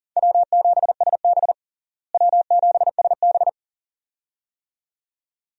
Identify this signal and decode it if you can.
Morse code, heard as a keyed tone, about 37 wpm, 700 Hz. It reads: W7SB W7SB
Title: USA Call Signs - 4 Characters - Repeated 2x - 40wpm